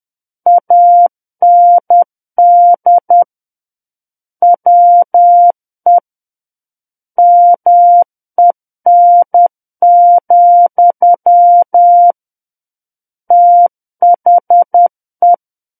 Morse code, heard as a keyed tone, about 10 wpm, 700 Hz. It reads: AND WE MEN, THE
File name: Book One / 01-Chapter 1 - 10 WPM